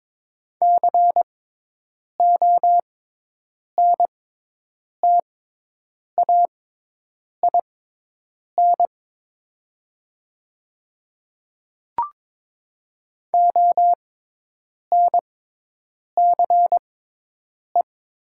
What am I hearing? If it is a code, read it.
Morse code, heard as a keyed tone, about 21 wpm, spaced as for 8 wpm, 700 Hz. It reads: CONTAIN ONCE
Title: 6x Character Spacing - Top 500 Words - 22wpm